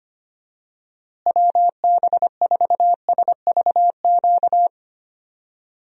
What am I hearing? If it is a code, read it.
Morse code, heard as a keyed tone, about 25 wpm, 700 Hz. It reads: WB4SVQ
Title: USA Call Signs - 6 Characters - Repeated 3x - 25wpm